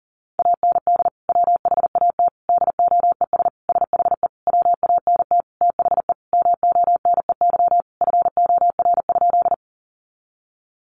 Morse code, heard as a keyed tone, about 40 wpm, 700 Hz. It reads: AND WHAT DOES SHE WANT THE MONEY FOR?
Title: Sentences from Top 400 Words 40wpm